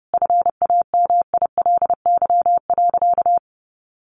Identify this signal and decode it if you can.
Morse code, heard as a keyed tone, about 30 wpm, 700 Hz. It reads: FAMILY.
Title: Sentences from Top 1400 Words 30wpm